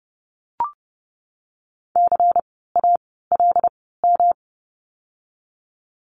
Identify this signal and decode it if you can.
Morse code, heard as a keyed tone, about 30 wpm, spaced as for 17 wpm, 700 Hz. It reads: CALM
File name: course / 30:10wpm / 108 Lesson 18f - Letter m - Words 30wpm